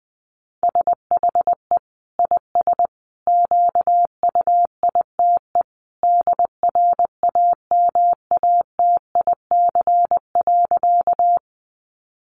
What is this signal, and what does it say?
Morse code, heard as a keyed tone, about 20 wpm, 700 Hz. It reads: SHE IS QUITE DRAMATIC.
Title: Sentences from Top 1300 Words 20wpm